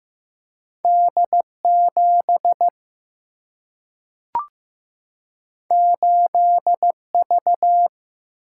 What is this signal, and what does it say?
Morse code, heard as a keyed tone, about 15 wpm, 700 Hz. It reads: D7 8V
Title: Single-Letter Single-Number 15wpm